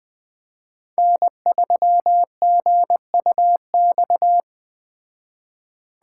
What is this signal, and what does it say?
Morse code, heard as a keyed tone, about 19 wpm, 700 Hz. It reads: N3GUX